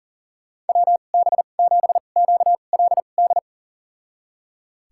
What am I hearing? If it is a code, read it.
Morse code, heard as a keyed tone, about 37 wpm, spaced as for 28 wpm, 700 Hz. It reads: WB7QLD
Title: USA Call Signs - 6 Characters - 2x Character Spacing - 40wpm